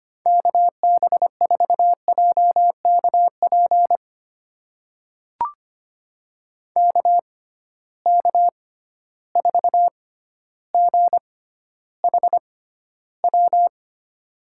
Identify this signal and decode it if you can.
Morse code, heard as a keyed tone, about 25 wpm, spaced as for 13 wpm, 700 Hz. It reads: KB4JKP KK4GHW
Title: USA Call Signs - 6 Characters - 6x Character Spacing - 25wpm